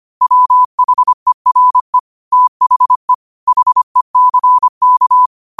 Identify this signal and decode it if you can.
Morse code, heard as a keyed tone, about 25 wpm, 1000 Hz. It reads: WHERE THE HECK